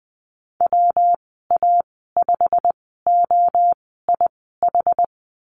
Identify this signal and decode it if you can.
Morse code, heard as a keyed tone, about 20 wpm, spaced as for 14 wpm, 700 Hz. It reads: WA5OIH
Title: 128 Lesson 21g - Number 6 - Call Signs 20wpm